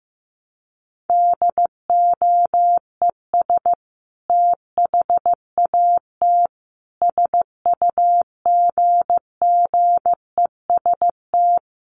Morse code, heard as a keyed tone, about 15 wpm, 700 Hz. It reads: DOES THAT SUGGEST
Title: Sentences from Top 1300 Words 15wpm